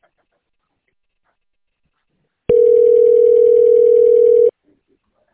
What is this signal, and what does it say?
Ringback tone (North America), 440 + 480 Hz.